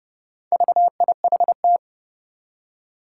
Morse code, heard as a keyed tone, about 30 wpm, 700 Hz. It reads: VIHT
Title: Four Challenging Letter-Number Combinations 30wpm